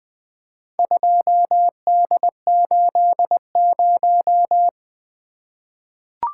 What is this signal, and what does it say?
Morse code, heard as a keyed tone, about 20 wpm, 700 Hz. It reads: 2D80
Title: Four Letter-Number Combinations - 20wpm